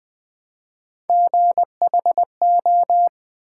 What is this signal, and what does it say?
Morse code, heard as a keyed tone, about 20 wpm, 700 Hz. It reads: GHO
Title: Build A Word - 1 Letter At A Time - 5-Letter Words - 20wpm